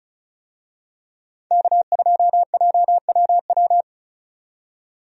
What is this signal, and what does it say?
Morse code, heard as a keyed tone, about 36 wpm, 700 Hz. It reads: K2JWW
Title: USA Call Signs - 5 Characters - 35wpm